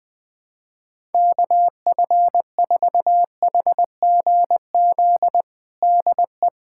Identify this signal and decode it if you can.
Morse code, heard as a keyed tone, about 21 wpm, 700 Hz. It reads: KF4HGZ DE
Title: USA Callsign DE Callsign - 6 Characters - 20wpm